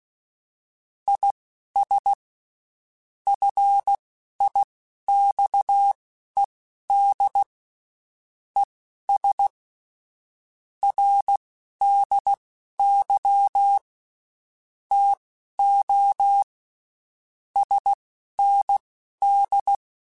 Morse code, heard as a keyed tone, about 16 wpm, spaced as for 11 wpm, 800 Hz. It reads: IS FIXED ES RDY TO SND
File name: (spaced) / QSOs Spaced-mp3s / WA7QOX 1 @15 sp